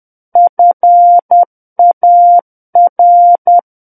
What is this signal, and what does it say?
Morse code, heard as a keyed tone, about 10 wpm, 700 Hz. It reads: FAR
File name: Book One / 01-Chapter 1 - 10 WPM